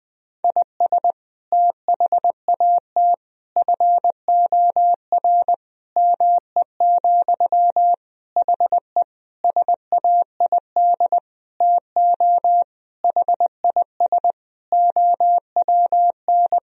Morse code, heard as a keyed tone, about 20 wpm, 700 Hz. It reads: IS THAT FOR ME, HE SAID TO HIS OWN
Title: Sentences from Top 1000 Words - Review Part 1 - 20wpm